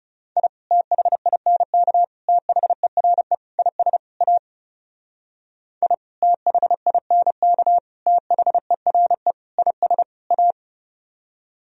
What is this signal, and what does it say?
Morse code, heard as a keyed tone, about 38 wpm, 700 Hz. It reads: I THINK THERE IS A I THINK THERE IS A